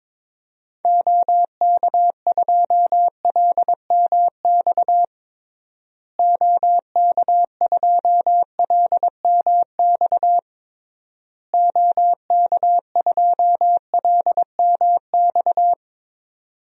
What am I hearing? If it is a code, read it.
Morse code, heard as a keyed tone, about 22 wpm, 700 Hz. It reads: OK2LMX OK2LMX OK2LMX
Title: DXCC-100 Call Signs - 6 Characters - Repeated 3x - 22wpm